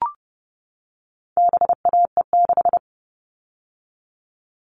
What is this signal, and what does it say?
Morse code, heard as a keyed tone, about 30 wpm, 700 Hz. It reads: BAE6
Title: All Cut Numbers - 4-Digit 30wpm